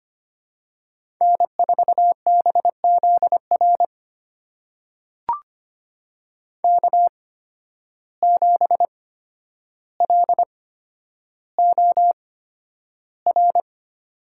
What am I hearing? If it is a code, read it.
Morse code, heard as a keyed tone, about 25 wpm, spaced as for 11 wpm, 700 Hz. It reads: N4BZR K7LOR
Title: USA Call Signs - 5 Characters - 8x Character Spacing - 25wpm